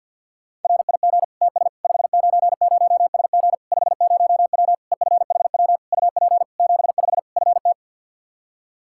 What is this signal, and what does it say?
Morse code, heard as a keyed tone, about 46 wpm, 700 Hz. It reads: RIG TS 590SG 50W EFHW UP 75 FT
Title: Practice QSO - Exchange 2 - Info - 50wpm